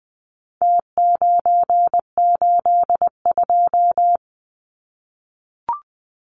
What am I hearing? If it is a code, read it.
Morse code, heard as a keyed tone, about 20 wpm, 700 Hz. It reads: T982
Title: Numbers - Contest Style - 4-Digit 20wpm